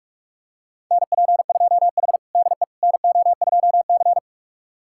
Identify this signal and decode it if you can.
Morse code, heard as a keyed tone, about 44 wpm, 700 Hz. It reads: NP2H DE NO2C
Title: USA Callsign DE Callsign - 4 Characters - 45wpm